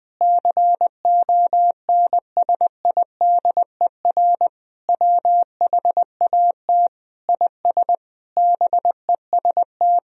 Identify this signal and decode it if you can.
Morse code, heard as a keyed tone, about 20 wpm, 700 Hz. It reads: CONSIDER WHAT IS BEST
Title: Sentences from Top 1000 Words - Review Part 4 - 20wpm